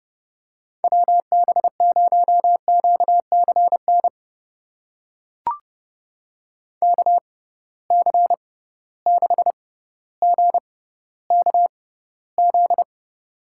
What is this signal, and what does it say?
Morse code, heard as a keyed tone, about 32 wpm, spaced as for 16 wpm, 700 Hz. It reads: WB0QCN KC6GKZ
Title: USA Call Signs - 6 Characters - 6x Character Spacing - 30wpm